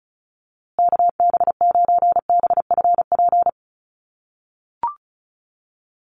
Morse code, heard as a keyed tone, about 34 wpm, 700 Hz. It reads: KB9BFP
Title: USA Call Signs Repeated 3x 35wpm